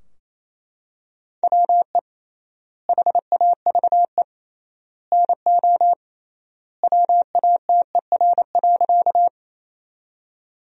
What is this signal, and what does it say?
Morse code, heard as a keyed tone, about 27 wpm, 700 Hz. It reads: WE HAVE NO WATER.
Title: 3x Word-Spacing - Sentences from Top 100 Words - 28wpm